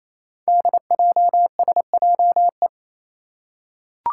Morse code, heard as a keyed tone, about 28 wpm, 700 Hz. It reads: DJSJE
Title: Five Challenging Letter-Number Combinations 28wpm